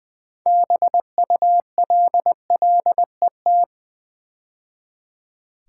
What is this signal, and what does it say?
Morse code, heard as a keyed tone, about 20 wpm, 700 Hz. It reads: BULLET
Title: 6 Letter Words - First 3 Letters Given - 20wpm